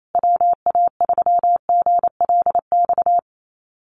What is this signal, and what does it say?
Morse code, heard as a keyed tone, about 28 wpm, 700 Hz. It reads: WA3GLX
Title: USA Call Signs Repeated 2x 28wpm